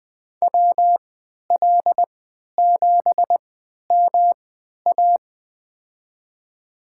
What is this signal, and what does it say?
Morse code, heard as a keyed tone, about 21 wpm, spaced as for 12 wpm, 700 Hz. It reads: WL7MA
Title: USA Call Signs - 3x Character Spacing - 20wpm